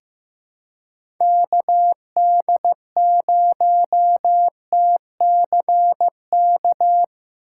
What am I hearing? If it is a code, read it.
Morse code, heard as a keyed tone, about 16 wpm, 700 Hz. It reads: KD0TCK
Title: USA Call Signs - 6 Characters - 2x Character Spacing - 15wpm